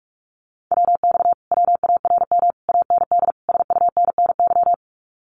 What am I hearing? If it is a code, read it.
Morse code, heard as a keyed tone, about 46 wpm, 700 Hz. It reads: WX WARM AND SUNNY
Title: Common QSO Phrases Encore 45wpm